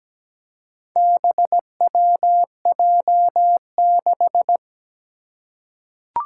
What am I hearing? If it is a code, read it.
Morse code, heard as a keyed tone, about 17 wpm, 700 Hz. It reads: BWJ6
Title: Four Challenging Letter-Number Combinations 17wpm